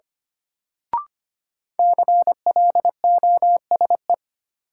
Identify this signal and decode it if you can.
Morse code, heard as a keyed tone, about 25 wpm, 700 Hz. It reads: CLOSE
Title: Sentences from Top 300 Words - New Word Drill - 25wpm